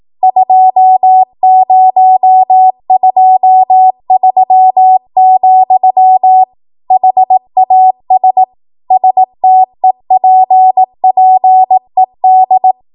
Morse code, heard as a keyed tone, about 18 wpm, 750 Hz. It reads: 2023, HAS STEPPED
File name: Archive / 18 WPM / 18WPM